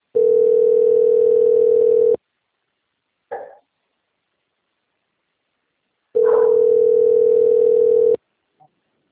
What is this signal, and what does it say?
Ringback tone (North America), 440 + 480 Hz.
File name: out- - - - - .61368